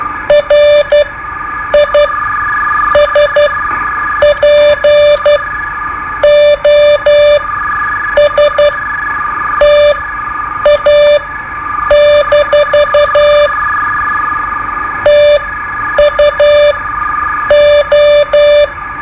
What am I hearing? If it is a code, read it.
Morse code, heard as a keyed tone, about 12 wpm, spaced as for 7 wpm, 590 Hz. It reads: RISPOSTA- TUO
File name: qso-01